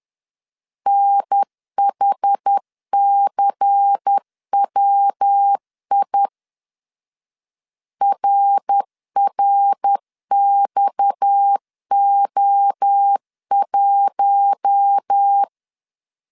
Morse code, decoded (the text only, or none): NHCWI RRXO1